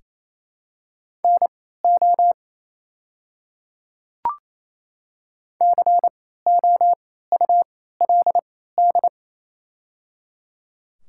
Morse code, heard as a keyed tone, about 28 wpm, spaced as for 16 wpm, 700 Hz. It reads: NO COULD
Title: 3x Character Spacing - Top 100 Words - 28wpm